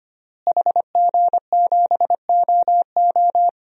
Morse code, heard as a keyed tone, about 25 wpm, 700 Hz. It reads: HG7OO
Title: DXCC-100 Call Signs - 5 Characters - Repeated 3x - 25wpm